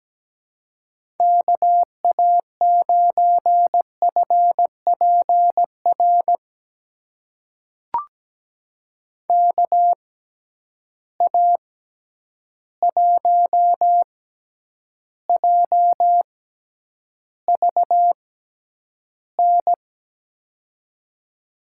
Morse code, heard as a keyed tone, about 17 wpm, spaced as for 9 wpm, 700 Hz. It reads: KA9FPR KA1JVN
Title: USA Call Signs - 6 Characters - 6x Character Spacing - 17wpm